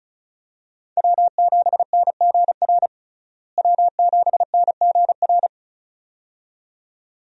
Morse code, heard as a keyed tone, about 34 wpm, 700 Hz. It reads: W7NGR W7NGR
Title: USA Call Signs - 5 Characters - Repeated 2x - 35wpm